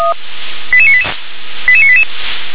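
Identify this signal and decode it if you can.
Five-tone selective call (ZVEI), heard as quick runs of five tones, tones of about 70 ms. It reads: 88988 88988
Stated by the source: ZVEI five-tone selective call received on an SDR.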